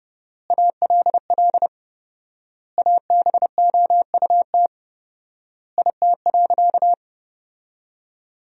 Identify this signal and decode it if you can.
Morse code, heard as a keyed tone, about 30 wpm, 700 Hz. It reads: ALL ABOUT IT.
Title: 4x Word-Spacing - Sentences from Top 400 Words - 30wpm